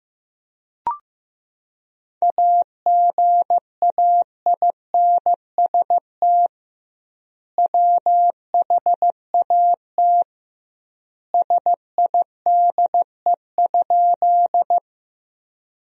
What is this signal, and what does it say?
Morse code, heard as a keyed tone, about 15 wpm, 700 Hz. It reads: AGAINST WHAT SIDE?
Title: 2x Word-Spacing - Sentences from Top 400 Words - 15wpm